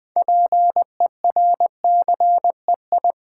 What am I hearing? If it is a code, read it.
Morse code, heard as a keyed tone, about 20 wpm, 700 Hz. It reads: PERCEI
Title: Build A Word - 3 Letters At A Time - 8-Letter Words - 20wpm